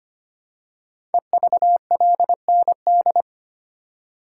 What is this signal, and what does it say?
Morse code, heard as a keyed tone, about 25 wpm, 700 Hz. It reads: EVLND
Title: Code Groups - 5 Letter:Numbers - 25wpm